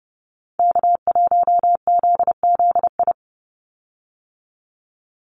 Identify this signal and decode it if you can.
Morse code, heard as a keyed tone, about 30 wpm, 700 Hz. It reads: K1ZZI
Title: Call Signs - 30wpm